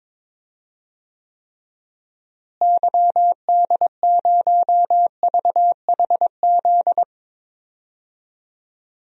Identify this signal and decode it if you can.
Morse code, heard as a keyed tone, about 22 wpm, 700 Hz. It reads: YD0VHZ
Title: DXCC-100 Call Signs - 6 Characters - Repeated 2x - 22wpm